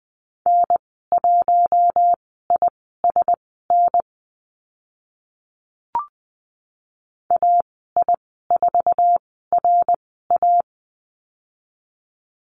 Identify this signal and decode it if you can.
Morse code, heard as a keyed tone, about 20 wpm, spaced as for 14 wpm, 700 Hz. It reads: N1ISN AI4RA
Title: 062 Lesson 11g - Letter h - Call Signs 20wpm